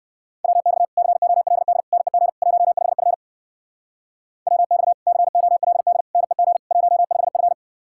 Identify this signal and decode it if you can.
Morse code, heard as a keyed tone, about 64 wpm, 700 Hz. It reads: WX COLD NEG 15F WX COLD NEG 15F